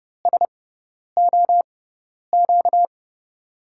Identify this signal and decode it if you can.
Morse code, heard as a keyed tone, about 30 wpm, spaced as for 10 wpm, 690 Hz. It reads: SOQ
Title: USA Call Signs - 5 Characters - 6x Character Spacing - 30wpm